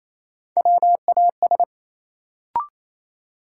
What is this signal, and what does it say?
Morse code, heard as a keyed tone, about 28 wpm, 700 Hz. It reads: WAS
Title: Sentences from Top 100 Words - New Word Drill - 28wpm